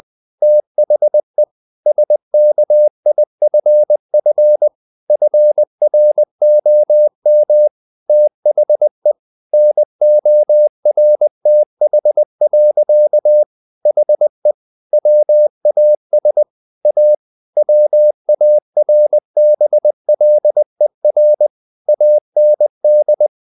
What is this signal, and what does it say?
Morse code, heard as a keyed tone, about 20 wpm, 600 Hz. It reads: THE SKIFF FROM THE NORTH. HE WAS A WARBLER AND
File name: the old man and the sea / the old man and the sea part 1 20wpm